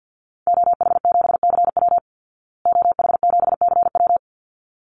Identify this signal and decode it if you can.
Morse code, heard as a keyed tone, about 48 wpm, 700 Hz. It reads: OH7CW OH7CW
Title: Call Signs Repeated 3x - 50wpm